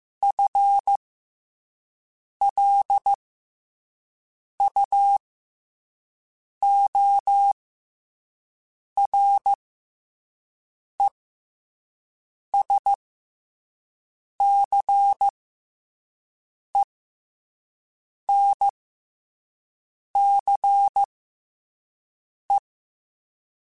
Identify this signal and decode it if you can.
Morse code, heard as a keyed tone, about 15 wpm, spaced as for 5 wpm, 800 Hz. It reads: FLUORESCENCE
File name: FL Runs / FL Runs mp3s / FL Runs @5